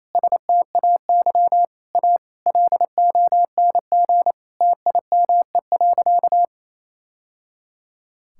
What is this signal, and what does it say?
Morse code, heard as a keyed tone, about 28 wpm, 700 Hz. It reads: STAY A LONG TIME.
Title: Sentences from Top 1000 Words - Review Part 6 - 28wpm